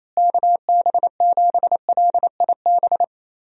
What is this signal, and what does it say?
Morse code, heard as a keyed tone, about 28 wpm, 700 Hz. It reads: KB7LIB
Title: USA Call Signs - 6 Characters - Repeated 2x - 28wpm